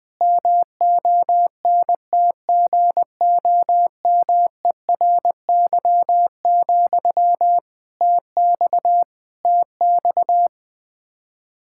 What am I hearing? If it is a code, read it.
Morse code, heard as a keyed tone, about 21 wpm, 700 Hz. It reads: MONTGOMERY, TX TX
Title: Practice QSO - Exchange 1 - Beginning - 20wpm